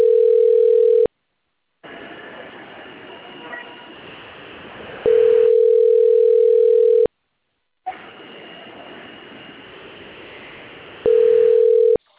Ringback tone (North America), 440 + 480 Hz.